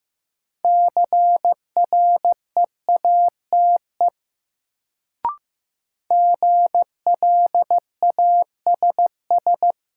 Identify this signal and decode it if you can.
Morse code, heard as a keyed tone, about 15 wpm, 700 Hz. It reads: CREATE GLASS
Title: Sentences from Top 1000 Words - Review Part 3 - New Word Drill - 15wpm